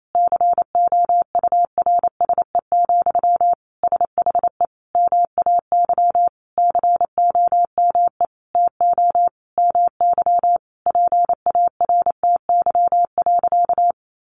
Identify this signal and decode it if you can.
Morse code, heard as a keyed tone, about 28 wpm, 700 Hz. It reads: COURSE, SHE MAY COME TO MY PARTY.
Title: Sentences from Top 1400 Words 28wpm